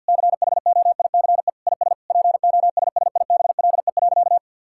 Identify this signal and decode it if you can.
Morse code, heard as a keyed tone, about 50 wpm, 700 Hz. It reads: CHOICE IS POSSIBLE.
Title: Sentences from Top 1000 Words - Review Part 6 - 50wpm